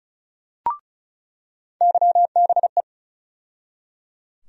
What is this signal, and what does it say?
Morse code, heard as a keyed tone, about 34 wpm, 700 Hz. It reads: YBE
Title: USA Call Sign Postfixes - 3 Letters 35wpm